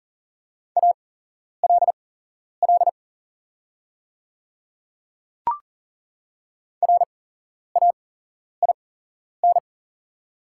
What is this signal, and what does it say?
Morse code, heard as a keyed tone, about 38 wpm, spaced as for 11 wpm, 700 Hz. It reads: ALL RAIN
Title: 8x Character Spacing - Top 500 Words - 40wpm